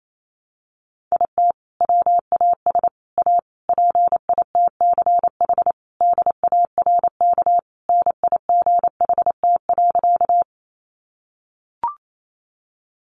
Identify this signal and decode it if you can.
Morse code, heard as a keyed tone, about 28 wpm, 700 Hz. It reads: IT WAS A PITCH DARK NIGHT.
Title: Sentences from Top 1300 Words 28wpm